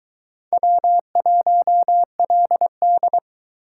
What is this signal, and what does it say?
Morse code, heard as a keyed tone, about 22 wpm, 700 Hz. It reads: W1LD